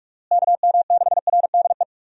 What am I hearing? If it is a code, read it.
Morse code, heard as a keyed tone, about 46 wpm, 700 Hz. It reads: KM6RDE
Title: USA Call Signs - 6 Characters - Repeated 2x - 45wpm